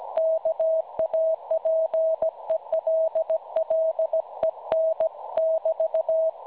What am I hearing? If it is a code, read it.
Morse code, heard as a keyed tone, about 17 wpm, 640 Hz. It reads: KAPELLEN <BT>
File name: cwronde / websdr recording .6kHz